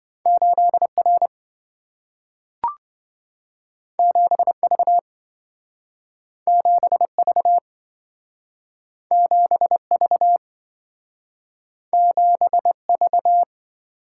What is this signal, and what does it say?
Morse code, heard as a keyed tone, about 27 wpm, 700 Hz. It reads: 8R 7V 7V 7V 7V